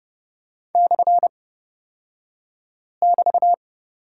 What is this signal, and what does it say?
Morse code, heard as a keyed tone, about 30 wpm, spaced as for 5 wpm, 690 Hz. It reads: /<BT>
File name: Lesson 36d - BT - Two Characters 30wpm